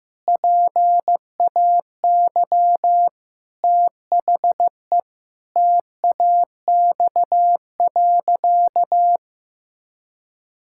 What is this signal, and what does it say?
Morse code, heard as a keyed tone, about 16 wpm, 700 Hz. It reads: PAY THE TAX.
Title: Sentences from Top 1000 Words - Review Part 6 - 15wpm